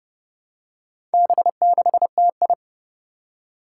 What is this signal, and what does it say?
Morse code, heard as a keyed tone, about 30 wpm, 700 Hz. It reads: B6TI
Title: Four Challenging Letter-Number Combinations 30wpm